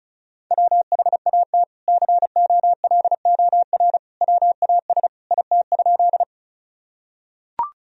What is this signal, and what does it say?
Morse code, heard as a keyed tone, about 36 wpm, 700 Hz. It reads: WHAT COLOR WAS IT?
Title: Sentences from Top 1000 Words - Review Part 2 - 35wpm